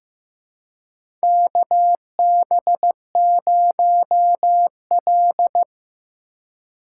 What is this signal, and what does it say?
Morse code, heard as a keyed tone, about 15 wpm, 700 Hz. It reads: KB0L
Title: USA Call Signs - 4 Characters - 2x Character Spacing - 15wpm